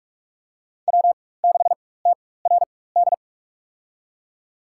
Morse code, heard as a keyed tone, about 46 wpm, spaced as for 21 wpm, 700 Hz. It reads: W6TRD
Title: USA Call Signs - 5 Characters - 4x Character Spacing - 45wpm